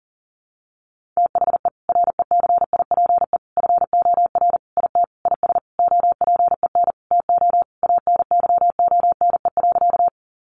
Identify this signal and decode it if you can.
Morse code, heard as a keyed tone, about 40 wpm, 700 Hz. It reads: THE RECIPE FOR IT IS OPEN TO ANYONE.
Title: Sentences from Top 1200 Words 40wpm